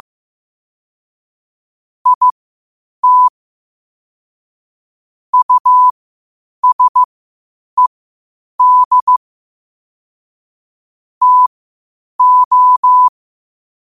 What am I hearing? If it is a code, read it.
Morse code, heard as a keyed tone, about 15 wpm, spaced as for 8 wpm, 1000 Hz. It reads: IT USED TO
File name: 8wpm